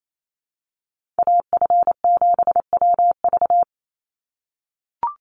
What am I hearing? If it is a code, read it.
Morse code, heard as a keyed tone, about 28 wpm, 700 Hz. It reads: AF7WV
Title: USA Call Signs 28wpm